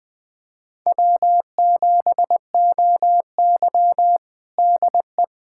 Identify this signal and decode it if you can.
Morse code, heard as a keyed tone, about 20 wpm, 700 Hz. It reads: W7OY DE
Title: USA Callsign DE Callsign - 4 Characters - 20wpm